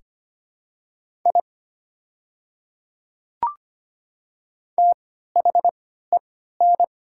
Morse code, heard as a keyed tone, about 25 wpm, spaced as for 14 wpm, 700 Hz. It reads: I THEN
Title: 3x Character Spacing - Top 100 Words - 25wpm